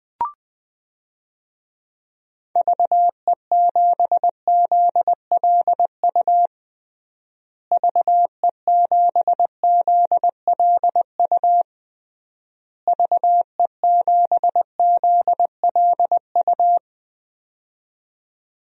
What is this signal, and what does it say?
Morse code, heard as a keyed tone, about 21 wpm, 700 Hz. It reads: VE7ZLU VE7ZLU VE7ZLU
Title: DXCC-100 Call Signs - 6 Characters - Repeated 3x - 20wpm